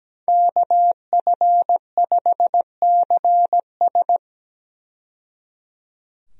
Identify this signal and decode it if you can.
Morse code, heard as a keyed tone, about 17 wpm, 700 Hz. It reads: KF5CS
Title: USA Call Signs - 5 Characters - 17wpm